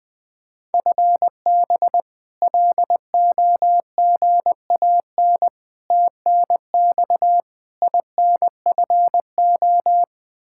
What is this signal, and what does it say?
Morse code, heard as a keyed tone, about 21 wpm, 700 Hz. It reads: FB LOGAN TNX INFO
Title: K1USN SST QSO Practice 20wpm